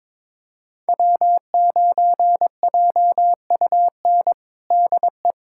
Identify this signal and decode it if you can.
Morse code, heard as a keyed tone, about 22 wpm, 700 Hz. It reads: W9JUN DE
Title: USA Callsign DE Callsign - 5 Characters - 22wpm